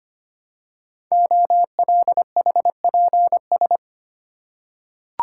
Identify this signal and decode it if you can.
Morse code, heard as a keyed tone, about 25 wpm, 700 Hz. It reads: OLHPS
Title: Code Groups - 5 Letters - 25wpm